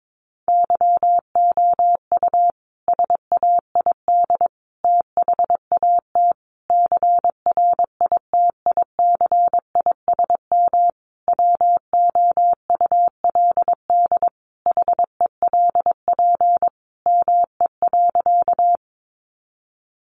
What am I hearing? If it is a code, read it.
Morse code, heard as a keyed tone, about 22 wpm, 700 Hz. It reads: YOU SAID THAT CRITICISM WOULD HELP ME.
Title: Sentences from Top 1300 Words 22wpm